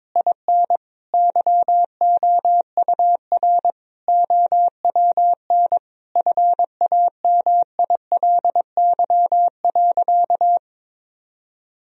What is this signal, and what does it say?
Morse code, heard as a keyed tone, about 22 wpm, 700 Hz. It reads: IN YOUR OWN FAMILY.
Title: Sentences from Top 1000 Words - Review Part 1 - 22wpm